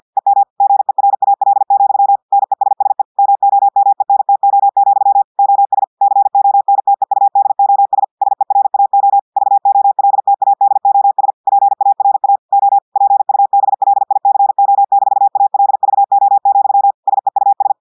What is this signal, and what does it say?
Morse code, heard as a keyed tone, about 50 wpm, 800 Hz. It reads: EM GERAL, NESSE MOMENTO, OS CONTEUDOS SERAO VOLTADOS PARA O PUBLICO-ALVO, SEUS
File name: audios / portugais / portugais texte en clair 2 vitesse 50 mots minutes